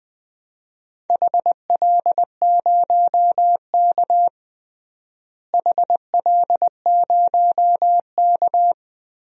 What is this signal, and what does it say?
Morse code, heard as a keyed tone, about 20 wpm, 700 Hz. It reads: HL0K HL0K
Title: DXCC-100 Call Signs - 4 Characters - Repeated 3x - 20wpm